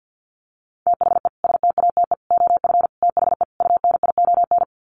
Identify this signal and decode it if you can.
Morse code, heard as a keyed tone, about 48 wpm, 700 Hz. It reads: THE STATE OF THE UNION
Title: Sets of 5 Words Encore 50wpm